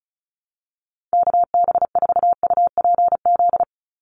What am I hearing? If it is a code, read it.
Morse code, heard as a keyed tone, about 34 wpm, 700 Hz. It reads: KB4UPZ
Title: USA Call Signs 35wpm